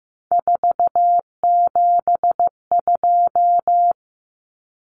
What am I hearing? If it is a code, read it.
Morse code, heard as a keyed tone, about 15 wpm, 700 Hz. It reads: 472
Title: Numbers - 3-Digit 15wpm